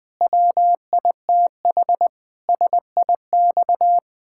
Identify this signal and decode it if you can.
Morse code, heard as a keyed tone, about 20 wpm, 700 Hz. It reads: WITH SIX
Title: Pangram Challenge v2 - 20wpm